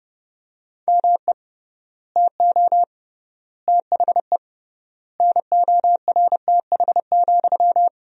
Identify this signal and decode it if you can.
Morse code, heard as a keyed tone, about 30 wpm, 700 Hz. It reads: ME TO THE NORTH,
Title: 3x Word-Spacing - Sentences from Top 400 Words - 30wpm